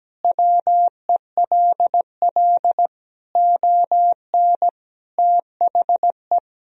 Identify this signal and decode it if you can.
Morse code, heard as a keyed tone, about 17 wpm, 700 Hz. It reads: WELL ON THE
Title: Sentences from Top 1000 Words - Review Part 1 - 17wpm